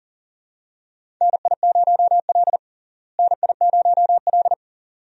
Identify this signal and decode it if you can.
Morse code, heard as a keyed tone, about 40 wpm, 700 Hz. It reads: NI0L NI0L
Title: USA Call Signs - 4 Characters - Repeated 2x - 40wpm